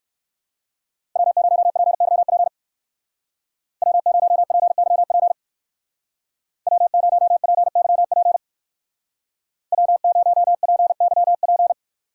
Morse code, heard as a keyed tone, about 56 wpm, 700 Hz. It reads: W0PYP W0PYP W0PYP W0PYP
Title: Speed Racing - USA 1x3 Call Signs 45wpm